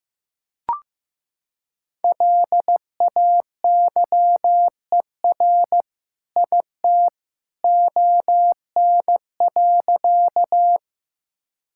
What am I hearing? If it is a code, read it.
Morse code, heard as a keyed tone, about 15 wpm, 700 Hz. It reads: LAYER IT ON.
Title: Sentences from Top 1600 Words 15wpm